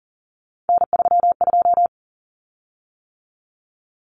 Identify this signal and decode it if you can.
Morse code, heard as a keyed tone, about 40 wpm, 700 Hz. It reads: N32
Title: Common Cut Numbers - 3-Digit 40wpm